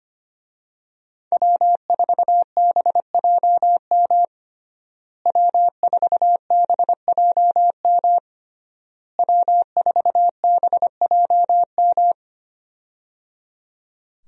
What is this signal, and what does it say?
Morse code, heard as a keyed tone, about 25 wpm, 700 Hz. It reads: W4BJM W4BJM W4BJM
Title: USA Call Signs - 5 Characters - Repeated 3x - 25wpm